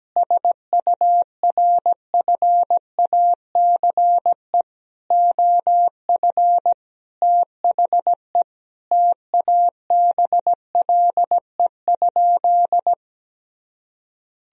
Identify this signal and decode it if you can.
Morse code, heard as a keyed tone, about 17 wpm, 700 Hz. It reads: SURFACE OF THE TABLE?
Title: Sentences from Top 1000 Words - Review Part 3 - 17wpm